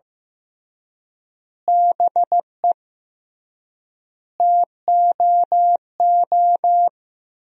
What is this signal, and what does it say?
Morse code, heard as a keyed tone, about 15 wpm, 700 Hz. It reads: BE TOO
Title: 3x Word-Spacing - Sentences from Top 300 Words - 15wpm